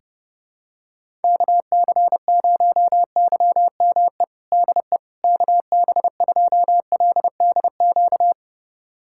Morse code, heard as a keyed tone, about 30 wpm, 700 Hz. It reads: KC0YME DE KB2LDQ
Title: USA Callsign DE Callsign - 6 Characters - 30wpm